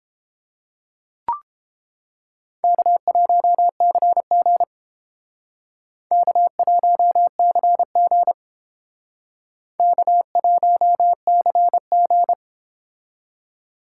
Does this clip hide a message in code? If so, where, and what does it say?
Yes: Morse code from 2.6 to 12.3 s, heard as a keyed tone, about 29 wpm, 700 Hz. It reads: K1CG K1CG K1CG